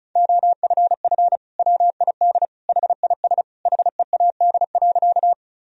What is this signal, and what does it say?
Morse code, heard as a keyed tone, about 36 wpm, 700 Hz. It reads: OFF WID HIS HEAD.
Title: Sentences from Top 300 Words and Abbreviations 35wpm